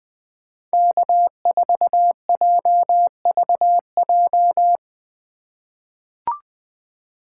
Morse code, heard as a keyed tone, about 20 wpm, 700 Hz. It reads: K4JVJ
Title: USA Call Signs - 5 Characters - 2x Character Spacing - 20wpm